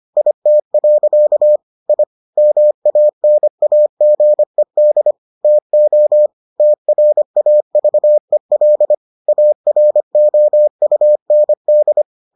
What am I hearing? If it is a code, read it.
Morse code, heard as a keyed tone, about 25 wpm, 600 Hz. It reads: IT. I MANAGED TO TRAVEL AROUND